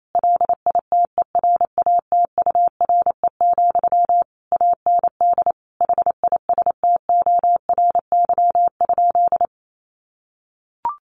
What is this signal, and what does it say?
Morse code, heard as a keyed tone, about 28 wpm, 700 Hz. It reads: LITERATURE, AND HISTORY?
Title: Sentences from Top 600 Words 28wpm